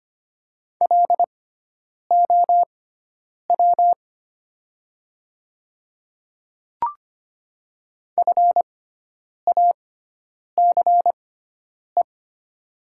Morse code, heard as a keyed tone, about 25 wpm, spaced as for 9 wpm, 700 Hz. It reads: LOW FACE
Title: 6x Character Spacing - Top 500 Words - 25wpm